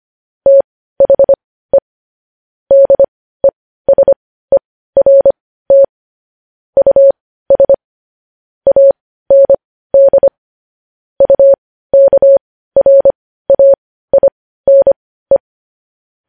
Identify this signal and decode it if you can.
Morse code, heard as a keyed tone, about 25 wpm, spaced as for 15 wpm, 550 Hz. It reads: THE DESERT US AND UKRAINE
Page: news-headlines-in-morse-code-at-15-wpm